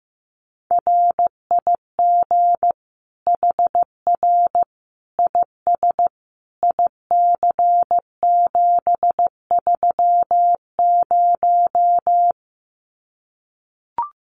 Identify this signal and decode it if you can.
Morse code, heard as a keyed tone, about 15 wpm, 700 Hz. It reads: RIG HR IS IC730
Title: Common QSO Phrases 15wpm